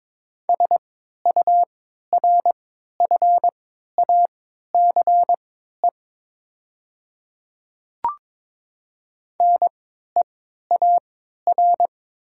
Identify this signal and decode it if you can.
Morse code, heard as a keyed tone, about 22 wpm, spaced as for 12 wpm, 690 Hz. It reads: SURFACE NEAR
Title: 3x Character Spacing - Top 500 Words - 22wpm